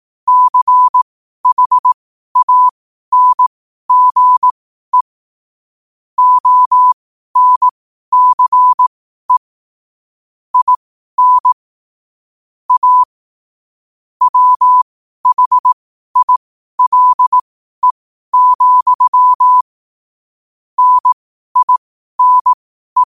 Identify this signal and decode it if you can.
Morse code, heard as a keyed tone, about 18 wpm, spaced as for 12 wpm, 1000 Hz. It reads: CHANGE ONCE IN A WHILE, NINE
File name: qotdfiles / 12wpm